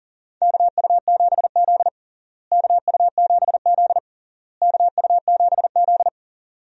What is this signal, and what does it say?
Morse code, heard as a keyed tone, about 38 wpm, 700 Hz. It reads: KU7Z KU7Z KU7Z
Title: USA Call Signs - 4 Characters - Repeated 3x - 40wpm